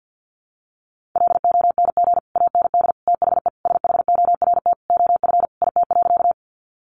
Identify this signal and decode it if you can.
Morse code, heard as a keyed tone, about 50 wpm, 700 Hz. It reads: LONG AND THE SHORT OF IT.
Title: Sentences from Top 400 Words 50wpm